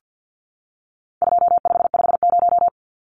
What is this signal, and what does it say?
Morse code, heard as a keyed tone, about 50 wpm, 700 Hz. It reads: 2550
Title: Numbers - 4-Digit 50wpm